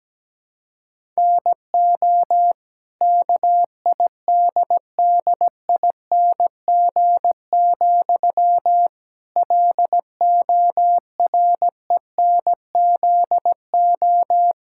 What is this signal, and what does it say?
Morse code, heard as a keyed tone, about 17 wpm, 700 Hz. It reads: NO KIDDING, LORENZO
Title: Pangram Challenge v2 - 17wpm